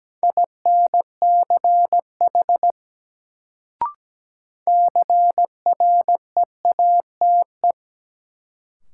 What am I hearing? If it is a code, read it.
Morse code, heard as a keyed tone, about 17 wpm, 700 Hz. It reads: INCH CREATE
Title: Sentences from Top 1000 Words - Review Part 3 - New Word Drill - 17wpm